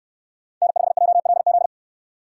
Morse code, heard as a keyed tone, about 63 wpm, 700 Hz. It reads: N5JLZ